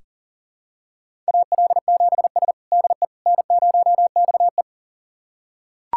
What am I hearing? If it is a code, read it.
Morse code, heard as a keyed tone, about 40 wpm, 700 Hz. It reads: AL7S DE N0XE
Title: USA Callsign DE Callsign - 4 Characters - 40wpm